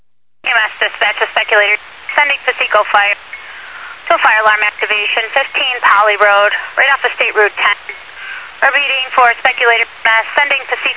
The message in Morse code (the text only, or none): none